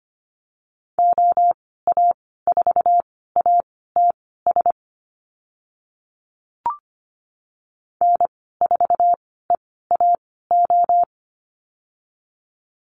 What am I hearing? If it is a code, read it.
Morse code, heard as a keyed tone, about 25 wpm, spaced as for 16 wpm, 700 Hz. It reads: OA4ATS N4EAO
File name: Lesson 15g - Number 5 - Call Signs 25wpm